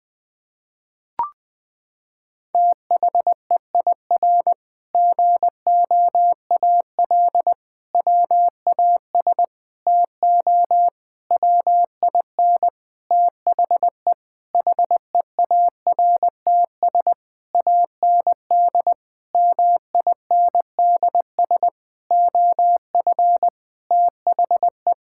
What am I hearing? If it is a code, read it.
Morse code, heard as a keyed tone, about 20 wpm, 700 Hz. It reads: THEIR GOAL WAS TO WIN THE HEARTS AND MINDS OF THE
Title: Sentences with 8-Word Phrases - 20wpm